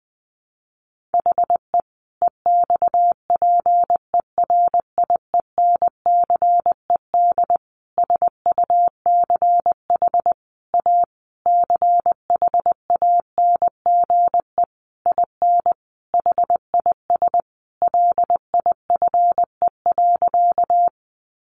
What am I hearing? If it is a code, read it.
Morse code, heard as a keyed tone, about 20 wpm, 700 Hz. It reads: HE EXPERIENCED SUCH A CHANGE IN HIS LIFE.
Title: Sentences from Top 1100 Words 20wpm